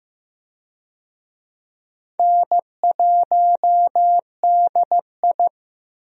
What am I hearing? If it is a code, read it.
Morse code, heard as a keyed tone, about 15 wpm, 700 Hz. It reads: N1DI
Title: USA Call Signs - 4 Characters - 15wpm